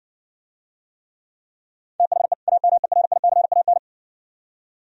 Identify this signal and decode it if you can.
Morse code, heard as a keyed tone, about 60 wpm, 700 Hz. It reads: THE AMERICAN